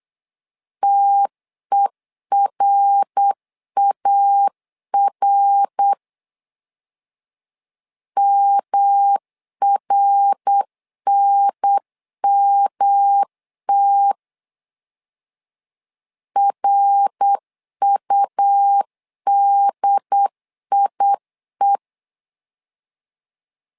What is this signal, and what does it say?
Morse code, heard as a keyed tone, about 8 wpm, 800 Hz. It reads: TERAR MRNMT RUDIE
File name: LEZIONI / 3 MWRO / 3 MWR 8wpm 2444